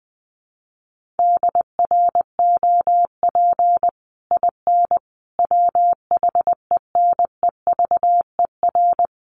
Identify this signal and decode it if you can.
Morse code, heard as a keyed tone, about 20 wpm, 700 Hz. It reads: DROP IN WHENEVER
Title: Sentences from Top 1400 Words 20wpm